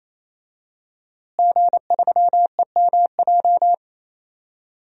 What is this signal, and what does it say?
Morse code, heard as a keyed tone, about 28 wpm, 700 Hz. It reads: G3EMJ
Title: DXCC-100 Call Signs - 5 Characters - Repeated 3x - 28wpm